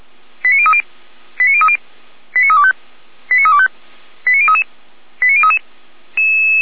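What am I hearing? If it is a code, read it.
Five-tone selective call (ZVEI), heard as quick runs of five tones, tones of about 70 ms. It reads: 89030 89030 89325 89325 89033 89033
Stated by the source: ZVEI five-tone call received by SDR, logged by Feuerwehr Orlamünde fire brigade.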